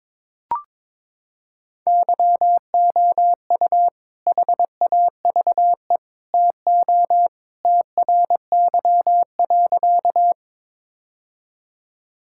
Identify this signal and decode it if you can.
Morse code, heard as a keyed tone, about 21 wpm, 700 Hz. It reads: YOU HAVE TO TRY.
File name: practice / 22wpm / Sentences with 3-Word Phrases - 22wpm